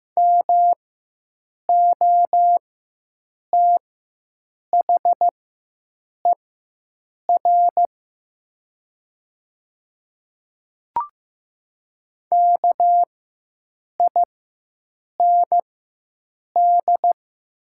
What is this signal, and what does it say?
Morse code, heard as a keyed tone, about 15 wpm, spaced as for 7 wpm, 700 Hz. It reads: MOTHER KIND
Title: 4x Character Spacing - Top 200 Words - 15wpm